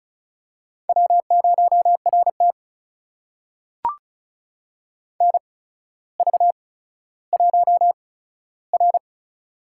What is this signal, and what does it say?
Morse code, heard as a keyed tone, about 36 wpm, spaced as for 15 wpm, 700 Hz. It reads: W0RT NV1R
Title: USA Call Signs - 4 Characters - 8x Character Spacing - 35wpm